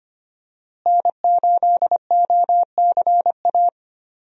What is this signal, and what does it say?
Morse code, heard as a keyed tone, about 25 wpm, 700 Hz. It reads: N8OCA
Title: USA Call Signs - 5 Characters - Repeated 3x - 25wpm